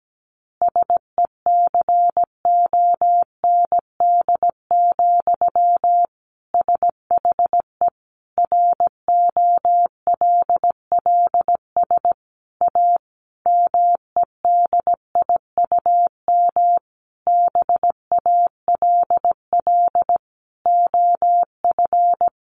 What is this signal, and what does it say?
Morse code, heard as a keyed tone, about 17 wpm, 700 Hz. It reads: SECOND, SHE ROLLS A MEDIUM BALL OF
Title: Easy Phrases 17wpm